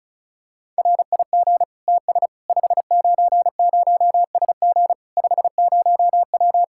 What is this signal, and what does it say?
Morse code, heard as a keyed tone, about 36 wpm, 700 Hz. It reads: RIG TS 590SG 50W
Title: Practice QSO - Exchange 2 - Info - 35wpm